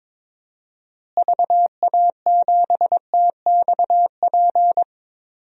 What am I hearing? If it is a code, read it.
Morse code, heard as a keyed tone, about 22 wpm, 700 Hz. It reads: VA7TXP
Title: DXCC-100 Call Signs - 6 Characters - Repeated 2x - 22wpm